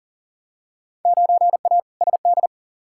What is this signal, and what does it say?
Morse code, heard as a keyed tone, about 40 wpm, 700 Hz. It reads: 9A SD
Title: Field Day Exchanges Encore - 40wpm